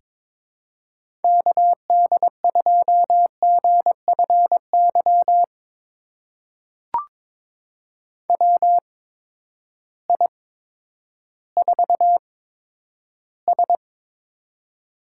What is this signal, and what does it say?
Morse code, heard as a keyed tone, about 22 wpm, spaced as for 11 wpm, 700 Hz. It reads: KD2GFY WI4S
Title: USA Call Signs - 8x Character Spacing - 22wpm